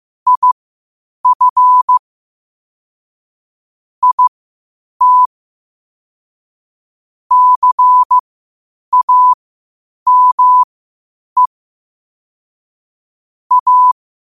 Morse code, heard as a keyed tone, about 15 wpm, spaced as for 8 wpm, 1000 Hz. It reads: IF IT CAME A
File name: qotdfiles / 8wpm